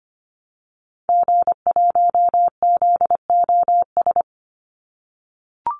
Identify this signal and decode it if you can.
Morse code, heard as a keyed tone, about 25 wpm, 700 Hz. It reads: G1ZOS
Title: Call Signs - 25wpm